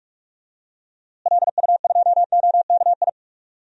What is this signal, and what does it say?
Morse code, heard as a keyed tone, about 45 wpm, 700 Hz. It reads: LU2OKI
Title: DXCC-100 Call Signs - 6 Characters - Repeated 3x - 45wpm